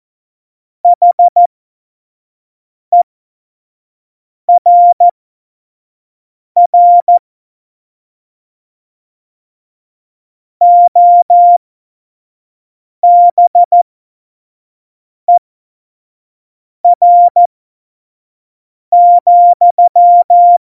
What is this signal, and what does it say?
Morse code, heard as a keyed tone, about 14 wpm, spaced as for 5 wpm, 700 Hz. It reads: HERR OBER,
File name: Morsekurs / Witze / Witz 10. September 2019, 25 F70